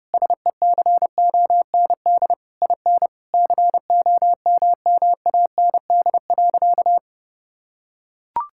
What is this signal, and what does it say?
Morse code, heard as a keyed tone, about 32 wpm, 700 Hz. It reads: SECOND IN COMMAND.
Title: Sentences from Top 1600 Words 30wpm